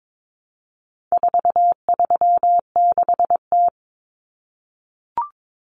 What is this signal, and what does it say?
Morse code, heard as a keyed tone, about 22 wpm, 700 Hz. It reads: 436T
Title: Common Cut Numbers - 4-Digit 22wpm